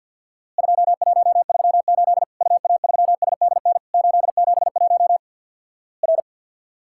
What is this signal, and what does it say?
Morse code, heard as a keyed tone, about 50 wpm, 700 Hz. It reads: 2138 UA3SDN 871 R
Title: Worked All Europe DX Contest - QTC Exchange with Repeats - 50wpm